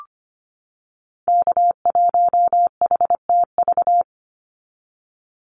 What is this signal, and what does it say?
Morse code, heard as a keyed tone, about 25 wpm, 700 Hz. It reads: K1HTV
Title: Call Signs - 25wpm